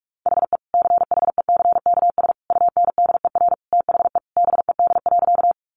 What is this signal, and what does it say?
Morse code, heard as a keyed tone, about 44 wpm, 700 Hz. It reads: HE CHECKS UNDER THE BED.
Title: Easy Phrases 45wpm